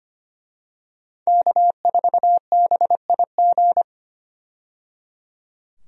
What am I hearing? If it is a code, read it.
Morse code, heard as a keyed tone, about 25 wpm, 700 Hz. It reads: K4BIG
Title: USA Call Signs - 5 Characters - 25wpm